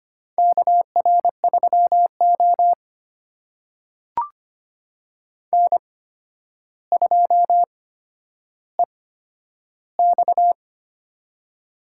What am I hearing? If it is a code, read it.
Morse code, heard as a keyed tone, about 25 wpm, spaced as for 11 wpm, 700 Hz. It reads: KR3O N2EX
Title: USA Call Signs - 4 Characters - 8x Character Spacing - 25wpm